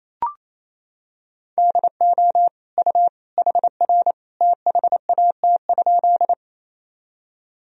Morse code, heard as a keyed tone, about 28 wpm, 700 Hz. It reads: DO U HR THAT?
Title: Sentences from Top 300 Words and Abbreviations 28wpm